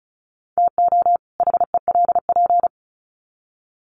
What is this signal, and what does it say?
Morse code, heard as a keyed tone, about 34 wpm, 700 Hz. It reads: TO HELP
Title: Sets of 2 Words 35wpm